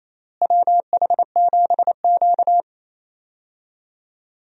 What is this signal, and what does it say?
Morse code, heard as a keyed tone, about 28 wpm, 700 Hz. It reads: WH7Q
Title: USA Call Signs - 4 Characters - Repeated 3x - 28wpm